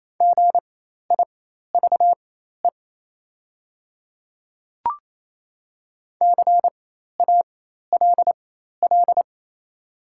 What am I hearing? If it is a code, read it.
Morse code, heard as a keyed tone, about 28 wpm, spaced as for 13 wpm, 690 Hz. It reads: GIVE CALL
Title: 4x Character Spacing - Top 200 Words - 28wpm